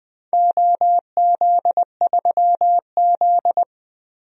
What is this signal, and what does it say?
Morse code, heard as a keyed tone, about 20 wpm, 700 Hz. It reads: OZ3Z
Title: DXCC-100 Call Signs - 4 Characters - 20wpm